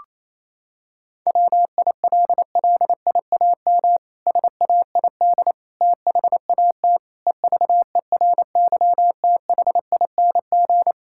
Morse code, heard as a keyed tone, about 28 wpm, 700 Hz. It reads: WILLIAM SAID THAT EVERYTHING
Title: Pangram Challenge v2 - 28wpm